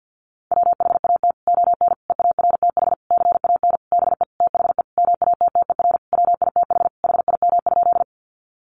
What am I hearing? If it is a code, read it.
Morse code, heard as a keyed tone, about 48 wpm, 700 Hz. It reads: WHAT ON EARTH CAN BE THE MATTER WITH HIM?
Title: Sentences from Top 700 Words 50wpm